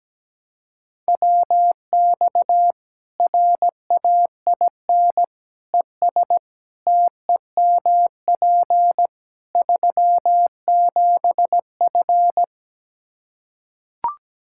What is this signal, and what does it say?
Morse code, heard as a keyed tone, about 17 wpm, 700 Hz. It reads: WX RAIN ES TEMP 37F
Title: Practice QSO - Exchange 2 - Info - 17wpm